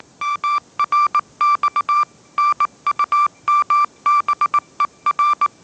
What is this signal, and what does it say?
Morse code, heard as a keyed tone, about 19 wpm, 1170 Hz. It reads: MRX NUMBER
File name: mrx083 20wpm